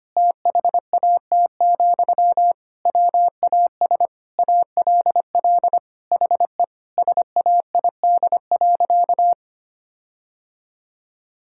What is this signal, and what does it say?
Morse code, heard as a keyed tone, about 25 wpm, 700 Hz. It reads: THAT, WAS ALL HE SAID.
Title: Sentences from Top 1600 Words 25wpm